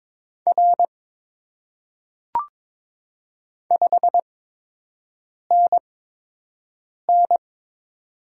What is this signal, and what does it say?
Morse code, heard as a keyed tone, about 22 wpm, spaced as for 6 wpm, 700 Hz. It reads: R 5NN
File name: Top 90 QSO Elements - 8x Character Spacing - 22wpm